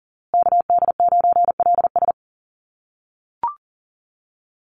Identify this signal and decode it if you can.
Morse code, heard as a keyed tone, about 40 wpm, 700 Hz. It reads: KD9LS
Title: USA Call Signs Repeated 2x 40wpm